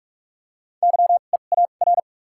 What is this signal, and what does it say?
Morse code, heard as a keyed tone, about 43 wpm, spaced as for 32 wpm, 700 Hz. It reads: YEAR
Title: 2x Character Spacing - Top 300 Words - 45wpm